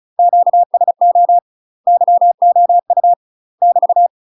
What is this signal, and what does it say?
Morse code, heard as a keyed tone, about 34 wpm, 700 Hz. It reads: QSO YOU <BT>
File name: UK MASTER QSOs TRAINING/35WPM